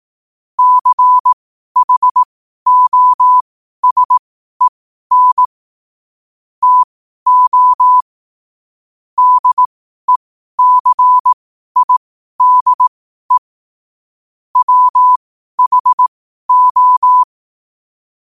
Morse code, heard as a keyed tone, about 18 wpm, spaced as for 12 wpm, 1000 Hz. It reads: CHOSEN TO DECIDE WHO
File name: qotdfiles / 12wpm